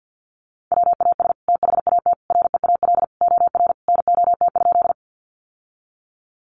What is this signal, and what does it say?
Morse code, heard as a keyed tone, about 50 wpm, 700 Hz. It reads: WAS THAT REAL OR NOT?
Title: Sentences from Top 600 Words 50wpm